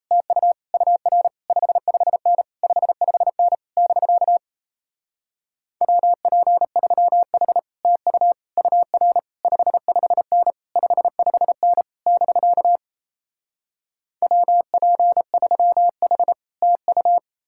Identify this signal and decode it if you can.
Morse code, heard as a keyed tone, about 38 wpm, 700 Hz. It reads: TU UR 55N 55N <BK> WP3H TU UR 55N 55N <BK> WP3H TU